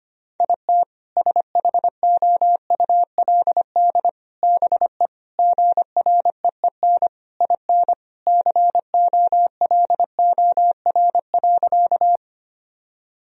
Morse code, heard as a keyed tone, about 25 wpm, 700 Hz. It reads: IT SHOULD BE GREEN IN COLOR.
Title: Sentences from Top 1000 Words - Review Part 2 - 25wpm